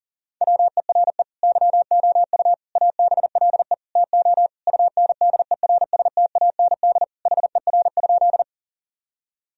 Morse code, heard as a keyed tone, about 40 wpm, 700 Hz. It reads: WERE YOU ABLE TO UNDERSTAND HER?
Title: Sentences from Top 1000 Words - Review Part 2 - 40wpm